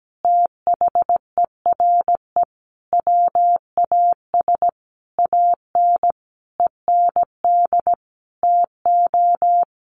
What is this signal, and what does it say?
Morse code, heard as a keyed tone, about 17 wpm, 700 Hz. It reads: THERE WAS AN END TO
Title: Sentences from Top 1400 Words 17wpm